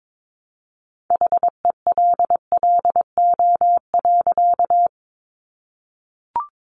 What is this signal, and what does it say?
Morse code, heard as a keyed tone, about 22 wpm, 700 Hz. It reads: HELLO.
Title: Sentences from Top 1100 Words 22wpm